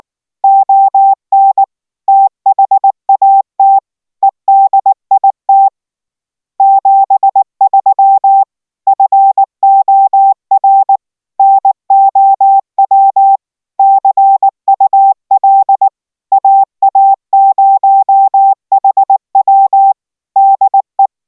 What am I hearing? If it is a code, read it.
Morse code, heard as a keyed tone, about 19 wpm, 780 Hz. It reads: ON THAT EDIT 73 FOR NOW CUL AA0HW DE